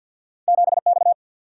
Morse code, heard as a keyed tone, about 48 wpm, 700 Hz. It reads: 7X
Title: Single-Letter Single-Number 50wpm